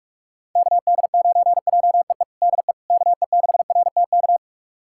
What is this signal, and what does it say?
Morse code, heard as a keyed tone, about 41 wpm, 700 Hz. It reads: KD9JEE DE KE6RTK
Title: USA Callsign DE Callsign - 6 Characters - 45wpm